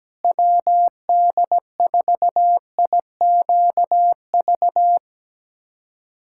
Morse code, heard as a keyed tone, about 17 wpm, 700 Hz. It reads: WD4IQV
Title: USA Callsign DE Callsign - 6 Characters - 17wpm